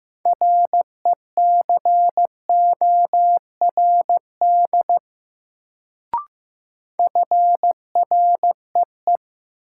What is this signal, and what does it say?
Morse code, heard as a keyed tone, about 15 wpm, 700 Hz. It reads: RECORD FREE
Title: Sentences from Top 500 Words - New Word Drill - 15wpm